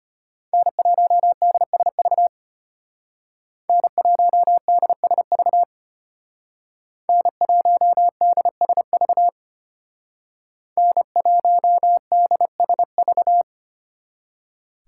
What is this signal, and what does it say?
Morse code, heard as a keyed tone, about 33 wpm, 700 Hz. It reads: N1DSV N1DSV N1DSV N1DSV